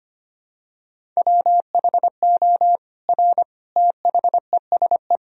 Morse code, heard as a keyed tone, about 25 wpm, 700 Hz. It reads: WHO R THESE
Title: Sentences from Top 200 Words and Abbreviations 25wpm